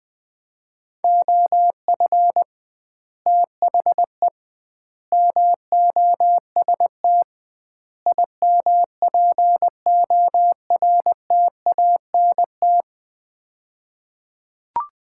Morse code, heard as a keyed tone, about 20 wpm, 700 Hz. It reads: OF THE MOST IMPORTANT
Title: 2x Word-Spacing - Sets of 4 Words - 20wpm